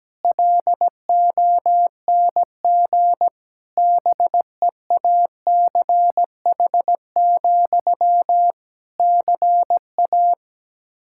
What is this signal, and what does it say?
Morse code, heard as a keyed tone, about 17 wpm, 700 Hz. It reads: LONG BEACH, CA
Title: Practice QSO - Exchange 1 - Beginning - 17wpm